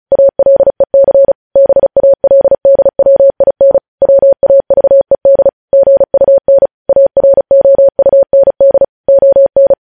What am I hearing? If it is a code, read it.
Morse code, heard as a keyed tone, about 36 wpm, 550 Hz. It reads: ALEC BALDWIN WAVED GUN AROUND ON
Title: News Headlines at 35